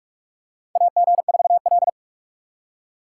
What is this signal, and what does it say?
Morse code, heard as a keyed tone, about 44 wpm, 690 Hz. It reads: AG4L
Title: USA Call Signs - 4 Characters - 4x Character Spacing - 45wpm